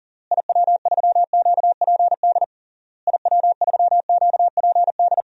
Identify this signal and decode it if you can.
Morse code, heard as a keyed tone, about 40 wpm, 700 Hz. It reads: IW3QPD IW3QPD
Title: DXCC-100 Call Signs - 6 Characters - Repeated 3x - 40wpm